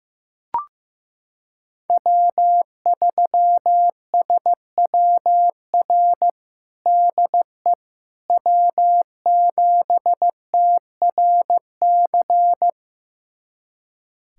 Morse code, heard as a keyed tone, about 15 wpm, 700 Hz. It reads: W3SWR DE W7TRC
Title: USA Callsign DE Callsign - 5 Characters - 15wpm